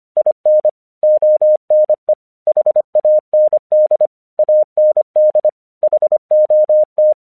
Morse code, heard as a keyed tone, about 25 wpm, 610 Hz. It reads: IN ONE HAND AND HOT
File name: 1 English 25wpm / AQC23